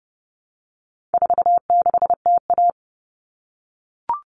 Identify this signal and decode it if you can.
Morse code, heard as a keyed tone, about 30 wpm, 700 Hz. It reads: 46TA
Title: All Cut Numbers - 4-Digit 30wpm